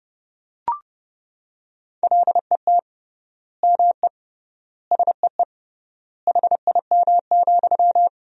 Morse code, heard as a keyed tone, about 32 wpm, 700 Hz. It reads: LET ME SEE HIM,
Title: 3x Word-Spacing - Sentences from Top 400 Words - 30wpm